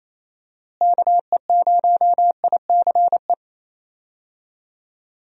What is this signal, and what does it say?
Morse code, heard as a keyed tone, about 28 wpm, 700 Hz. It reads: KE0ICE
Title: USA Call Signs - 6 Characters - Repeated 3x - 28wpm